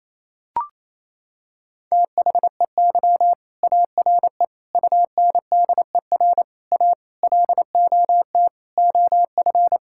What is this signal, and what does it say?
Morse code, heard as a keyed tone, about 28 wpm, 700 Hz. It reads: THEY ARE UNDER A LOT OF
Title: Sentences from Top 1000 Words - Review Part 6 - 28wpm